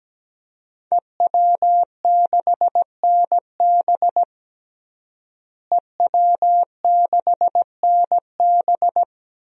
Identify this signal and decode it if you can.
Morse code, heard as a keyed tone, about 17 wpm, 700 Hz. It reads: EW6NB EW6NB
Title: DXCC-100 Call Signs - 5 Characters - Repeated 3x - 17wpm